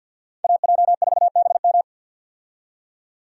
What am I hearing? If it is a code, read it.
Morse code, heard as a keyed tone, about 52 wpm, 700 Hz. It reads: AJ4BM
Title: USA Call Signs - 5 Characters - 50wpm